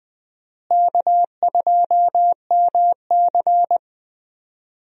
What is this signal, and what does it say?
Morse code, heard as a keyed tone, about 20 wpm, 700 Hz. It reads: K2MC
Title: USA Call Signs - 4 Characters - Repeated 2x - 20wpm